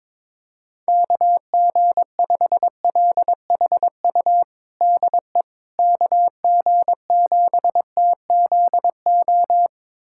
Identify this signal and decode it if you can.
Morse code, heard as a keyed tone, about 22 wpm, 700 Hz. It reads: KG5LHU DE KG7TZO
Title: USA Callsign DE Callsign - 6 Characters - 22wpm